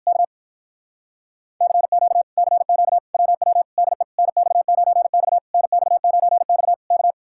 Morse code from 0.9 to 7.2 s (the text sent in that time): CQ CQ WW DE NX9X NX9X K